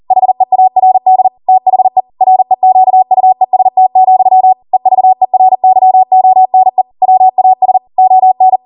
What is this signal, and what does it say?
Morse code, heard as a keyed tone, about 40 wpm, 750 Hz. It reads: HEARD THE REQUEST, EVERYONE WAS ON